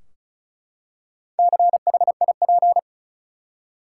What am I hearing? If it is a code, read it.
Morse code, heard as a keyed tone, about 36 wpm, 700 Hz. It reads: CHIP
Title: Four Letter Words - 35wpm